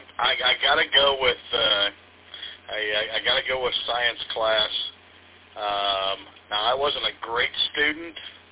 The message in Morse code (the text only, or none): none